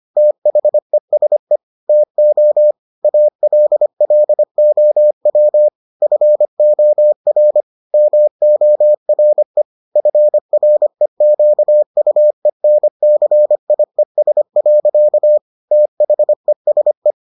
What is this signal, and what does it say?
Morse code, heard as a keyed tone, about 25 wpm, 600 Hz. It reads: THESE TO ALLOW FOR MORE FREQUENCIES. THESE